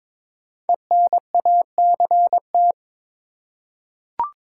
Morse code, heard as a keyed tone, about 22 wpm, 700 Hz. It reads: ENACT
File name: 5 Letter Words - First 3 Letters Given - 22wpm